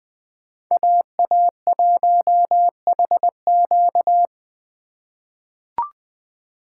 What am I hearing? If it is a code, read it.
Morse code, heard as a keyed tone, about 20 wpm, 700 Hz. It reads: AA1HQ
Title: USA Call Signs - 2x Character Spacing - 20wpm